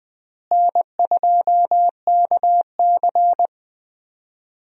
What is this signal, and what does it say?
Morse code, heard as a keyed tone, about 20 wpm, 700 Hz. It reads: N2KC
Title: USA Call Signs - 4 Characters - Repeated 3x - 20wpm